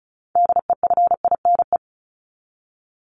Morse code, heard as a keyed tone, about 34 wpm, 700 Hz. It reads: DEFINE
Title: ED Suffix - Top 100 Words with Base Words 35wpm